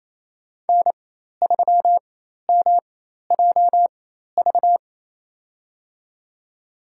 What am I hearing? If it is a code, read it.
Morse code, heard as a keyed tone, about 28 wpm, spaced as for 13 wpm, 700 Hz. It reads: N3MJV
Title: USA Call Signs - 5 Characters - 4x Character Spacing - 28wpm